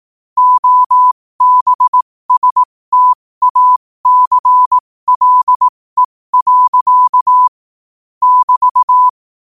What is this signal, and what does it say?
Morse code, heard as a keyed tone, about 18 wpm, spaced as for 15 wpm, 1000 Hz. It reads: OBSTACLE. <BT>
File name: qotdfiles / 15wpm